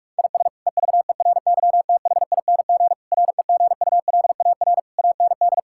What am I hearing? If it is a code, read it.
Morse code, heard as a keyed tone, about 46 wpm, 700 Hz. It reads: IS EVERYTHING REGULAR AND
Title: Sentences from Top 1000 Words - Review Part 6 - 45wpm